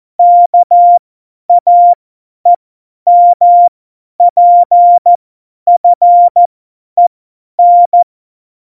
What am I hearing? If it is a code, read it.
Morse code, heard as a keyed tone, about 14 wpm, spaced as for 10 wpm, 700 Hz. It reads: KAEMPFEN
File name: Morsekurs / Witze / Witz 10 April 2018, 50 F70